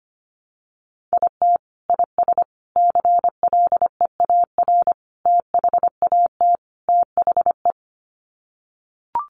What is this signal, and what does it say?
Morse code, heard as a keyed tone, about 25 wpm, 690 Hz. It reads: IT IS CLEAR THAT THE
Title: Sets of 5 Words Encore 25wpm